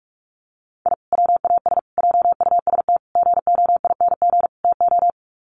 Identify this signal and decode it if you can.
Morse code, heard as a keyed tone, about 44 wpm, 700 Hz. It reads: I WAS JUST GOING TO
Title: Sets of 5 Words Encore 45wpm